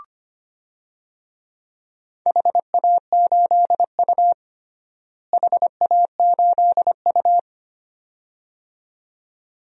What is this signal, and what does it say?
Morse code, heard as a keyed tone, about 25 wpm, 700 Hz. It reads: HA8U HA8U
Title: DXCC-100 Call Signs - 4 Characters - Repeated 2x - 25wpm